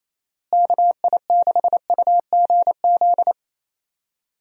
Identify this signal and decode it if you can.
Morse code, heard as a keyed tone, about 28 wpm, 700 Hz. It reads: KI6UGZ
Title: USA Call Signs - 6 Characters - Repeated 2x - 28wpm